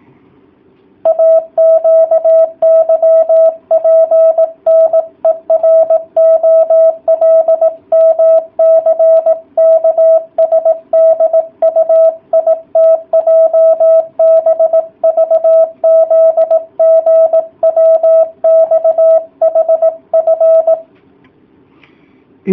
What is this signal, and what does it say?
Morse code, heard as a keyed tone, about 18 wpm, 650 Hz. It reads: AQYPNEROLMCKSDUITJBVZGWXHF